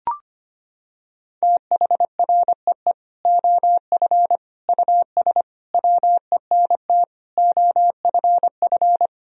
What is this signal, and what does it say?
Morse code, heard as a keyed tone, about 25 wpm, 700 Hz. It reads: THREE OF US WENT OFF
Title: Sentences from Top 1000 Words - Review Part 1 - 25wpm